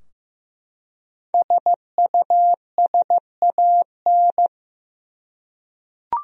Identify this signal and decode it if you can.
Morse code, heard as a keyed tone, about 15 wpm, 700 Hz. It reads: SUSAN
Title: USA Common Names - Max 5 Letters - 15wpm